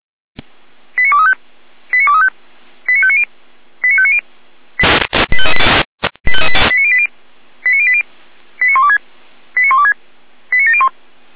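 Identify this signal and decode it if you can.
Five-tone selective call (ZVEI), heard as quick runs of five tones, tones of about 70 ms. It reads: 89235 89235 89590 89590 89080 89080 89126 89126 89701
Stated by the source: ZVEI five-tone selective call received on an SDR by Feuerwehr Orlamünde.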